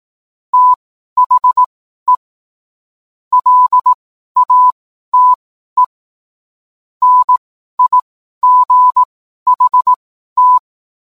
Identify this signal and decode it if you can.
Morse code, heard as a keyed tone, about 18 wpm, spaced as for 12 wpm, 1000 Hz. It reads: THE LATE NIGHT